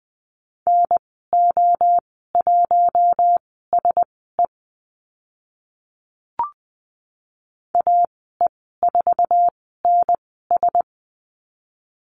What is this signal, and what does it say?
Morse code, heard as a keyed tone, about 20 wpm, spaced as for 14 wpm, 700 Hz. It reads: NO1SE AE4NS
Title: 062 Lesson 11g - Letter h - Call Signs 20wpm